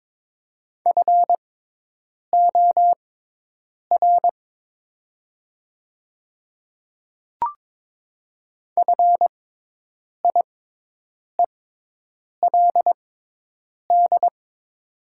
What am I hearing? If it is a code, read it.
Morse code, heard as a keyed tone, about 22 wpm, spaced as for 8 wpm, 700 Hz. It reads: FOR FIELD
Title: 6x Character Spacing - Top 500 Words - 22wpm